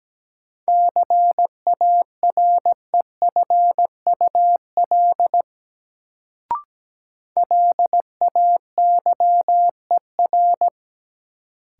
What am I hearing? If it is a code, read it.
Morse code, heard as a keyed tone, about 17 wpm, 700 Hz. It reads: CAREFUL LAYER
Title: Sentences from Top 1600 Words - New Word Drill - 17wpm